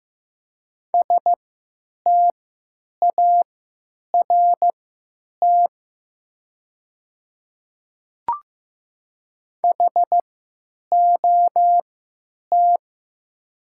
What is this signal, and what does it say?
Morse code, heard as a keyed tone, about 15 wpm, spaced as for 9 wpm, 700 Hz. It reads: START HOT
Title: 3x Character Spacing - Top 500 Words - 15wpm